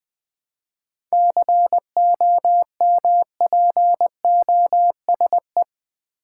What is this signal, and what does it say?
Morse code, heard as a keyed tone, about 20 wpm, 700 Hz. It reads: COMPOSE
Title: 7 Letter Words - First 3 Letters Given - 20wpm